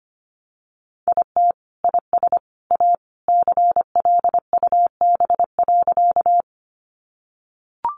Morse code, heard as a keyed tone, about 25 wpm, 700 Hz. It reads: IT IS A CLUB.
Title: Sentences from Top 1200 Words 25wpm